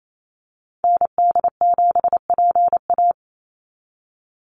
Morse code, heard as a keyed tone, about 28 wpm, 700 Hz. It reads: ND7PA
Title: NA SOTA Activator Callsigns 28wpm